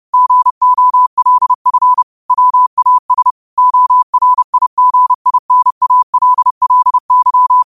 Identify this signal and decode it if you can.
Morse code, heard as a keyed tone, about 30 wpm, 1000 Hz. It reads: GOLF WAS ORIGINALLY
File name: qotdfiles / 30wpm